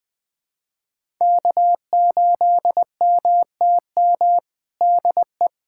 Morse code, heard as a keyed tone, about 20 wpm, 700 Hz. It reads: K8MTM DE
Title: USA Callsign DE Callsign - 5 Characters - 20wpm